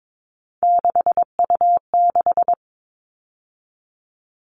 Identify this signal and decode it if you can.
Morse code, heard as a keyed tone, about 22 wpm, 700 Hz. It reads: 6U6
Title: All Cut Numbers - 3-Digit 22wpm